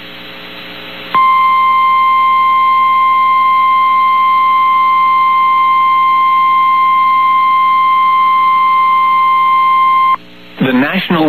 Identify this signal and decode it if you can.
Emergency Alert System (SAME) tones (the 1050 Hz attention tone), with speech. It is heard from 1.1 to 10.2 s.
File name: eas received alerts / L1 19504 audio